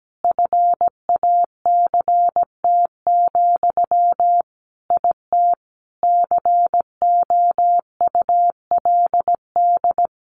Morse code, heard as a keyed tone, about 17 wpm, 700 Hz. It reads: FACT, IT COULD
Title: Sentences from Top 1200 Words 17wpm